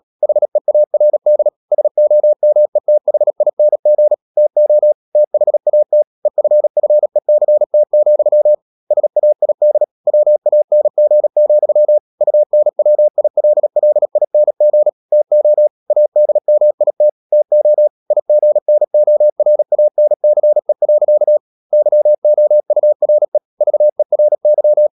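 Morse code, heard as a keyed tone, about 37 wpm, 600 Hz. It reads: HEARD SOMETHING TO THAT EFFECT, SAID WANG, UNWILLING TO ADMIT TO IGNORANCE. YOURE VERY
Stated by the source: Morse code practice at 35–39 wpm sending English text of Ken Follett's novel Never, from CW med Gnister.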